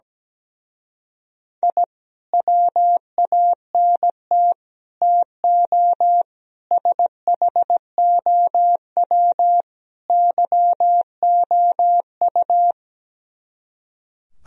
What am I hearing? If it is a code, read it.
Morse code, heard as a keyed tone, about 17 wpm, 700 Hz. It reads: I WANT TO SHOW YOU
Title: Speed-Racing - Sets of 5 Words 17wpm